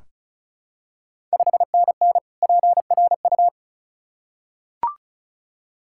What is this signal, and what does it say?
Morse code, heard as a keyed tone, about 36 wpm, 700 Hz. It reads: 5NN PRU
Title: Ontario QSO Party Exchange 35wpm